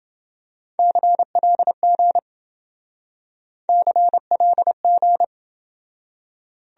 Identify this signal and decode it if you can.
Morse code, heard as a keyed tone, about 28 wpm, 700 Hz. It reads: CLG CLG